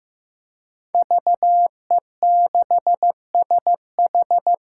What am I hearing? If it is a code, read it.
Morse code, heard as a keyed tone, about 16 wpm, 700 Hz. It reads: VE6SH
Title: K1USN SST QSO Practice - 15wpm